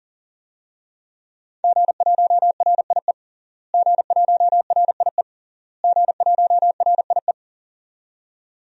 Morse code, heard as a keyed tone, about 37 wpm, 700 Hz. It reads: G1RIE G1RIE G1RIE
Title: DXCC-100 Call Signs - 5 Characters - Repeated 3x - 40wpm